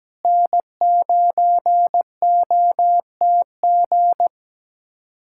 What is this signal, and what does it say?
Morse code, heard as a keyed tone, about 17 wpm, 700 Hz. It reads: N9OTG
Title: USA Callsign DE Callsign - 5 Characters - 17wpm